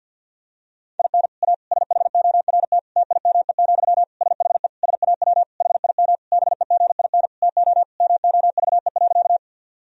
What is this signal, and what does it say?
Morse code, heard as a keyed tone, about 50 wpm, 700 Hz. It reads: IN A SHORT TIME, SHE SAW HIM BEGIN TO MOVE.
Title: Sentences from Top 1000 Words - Review Part 2 - 50wpm